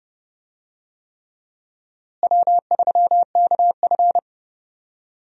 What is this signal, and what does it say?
Morse code, heard as a keyed tone, about 30 wpm, 700 Hz. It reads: W3KF
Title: USA Call Signs - 4 Characters - Repeated 3x - 30wpm